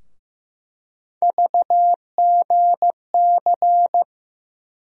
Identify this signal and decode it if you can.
Morse code, heard as a keyed tone, about 15 wpm, 700 Hz. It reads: VGC
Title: USA Call Sign Postfixes - 3 Letters 15wpm